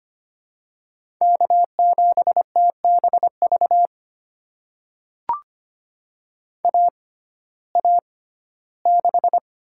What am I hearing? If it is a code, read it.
Morse code, heard as a keyed tone, about 25 wpm, spaced as for 15 wpm, 700 Hz. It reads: K7TBV AA6
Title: USA Call Signs - 5 Characters - 6x Character Spacing - 25wpm